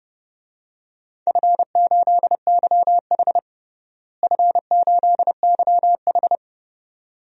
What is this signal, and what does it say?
Morse code, heard as a keyed tone, about 32 wpm, 700 Hz. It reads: F8YH F8YH
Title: DXCC-100 Call Signs - 4 Characters - Repeated 2x - 30wpm